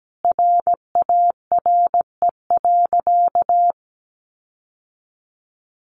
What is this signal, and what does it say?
Morse code, heard as a keyed tone, about 17 wpm, 700 Hz. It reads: RARE.
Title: Sentences from Top 1500 Words 17wpm